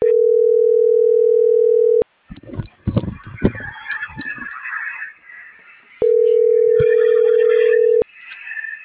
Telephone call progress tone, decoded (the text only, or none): ringback tone